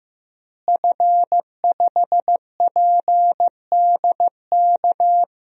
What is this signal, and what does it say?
Morse code, heard as a keyed tone, about 15 wpm, 700 Hz. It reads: F5PDK
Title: DXCC-100 Call Signs - 5 Characters - Repeated 3x - 15wpm